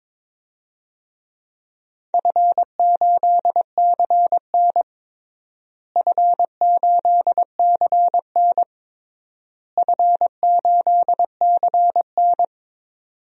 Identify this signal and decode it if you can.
Morse code, heard as a keyed tone, about 22 wpm, 700 Hz. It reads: F8CN F8CN F8CN
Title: DXCC-100 Call Signs - 4 Characters - Repeated 3x - 22wpm